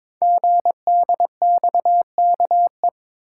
Morse code, heard as a keyed tone, about 22 wpm, 700 Hz. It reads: GDXKE
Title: Code Groups - 5 Letters - 22wpm